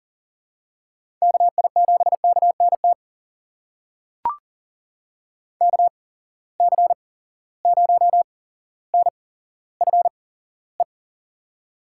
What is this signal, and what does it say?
Morse code, heard as a keyed tone, about 40 wpm, spaced as for 17 wpm, 700 Hz. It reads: KI7KNT KC0NFE
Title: USA Call Signs - 6 Characters - 8x Character Spacing - 40wpm